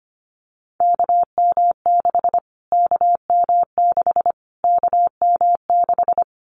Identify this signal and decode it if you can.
Morse code, heard as a keyed tone, about 25 wpm, 700 Hz. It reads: KM6 KM6 KM6
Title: Callsign Prefixes for Top 25 DXCC - Repeated 3x - 25wpm